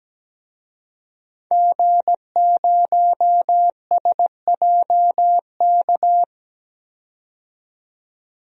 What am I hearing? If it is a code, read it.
Morse code, heard as a keyed tone, about 17 wpm, 700 Hz. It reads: G0SJK
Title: DXCC-100 Call Signs - 5 Characters - 17wpm